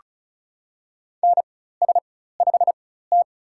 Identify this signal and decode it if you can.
Morse code, heard as a keyed tone, about 36 wpm, spaced as for 16 wpm, 700 Hz. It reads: NS5T
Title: USA Call Signs - 4 Characters - 4x Character Spacing - 35wpm